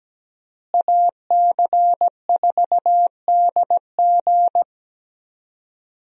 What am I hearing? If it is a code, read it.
Morse code, heard as a keyed tone, about 17 wpm, 700 Hz. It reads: AC4DG
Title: USA Call Signs - 5 Characters - Repeated 3x - 17wpm